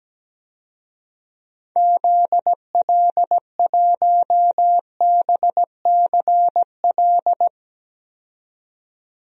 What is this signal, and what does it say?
Morse code, heard as a keyed tone, about 17 wpm, 700 Hz. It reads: ZL1BCL
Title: DXCC-100 Call Signs - 6 Characters - 17wpm